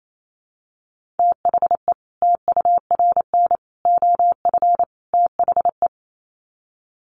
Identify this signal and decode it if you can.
Morse code, heard as a keyed tone, about 28 wpm, 700 Hz. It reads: THE TURN OF THE
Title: Sets of 4 Words 28wpm